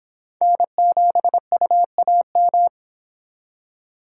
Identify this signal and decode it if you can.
Morse code, heard as a keyed tone, about 30 wpm, 700 Hz. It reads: N7UAM